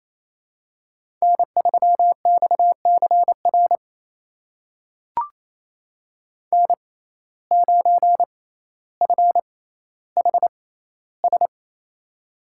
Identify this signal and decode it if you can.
Morse code, heard as a keyed tone, about 27 wpm, spaced as for 14 wpm, 700 Hz. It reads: N3XCR N9FHS
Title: USA Call Signs - 5 Characters - 6x Character Spacing - 28wpm